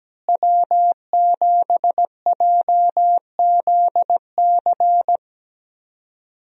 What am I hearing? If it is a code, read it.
Morse code, heard as a keyed tone, about 17 wpm, 700 Hz. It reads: W7JZC
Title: USA Call Signs - 5 Characters - Repeated 2x - 17wpm